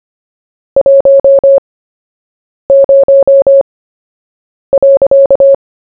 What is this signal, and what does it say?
Morse code, heard as a keyed tone, about 25 wpm, spaced as for 7 wpm, 550 Hz. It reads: 10.
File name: cw / fists / Short-Story-1-7wpm